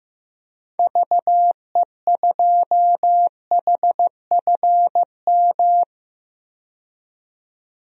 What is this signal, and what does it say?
Morse code, heard as a keyed tone, about 15 wpm, 700 Hz. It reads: VE2HFM
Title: DXCC-100 Call Signs - 6 Characters - 15wpm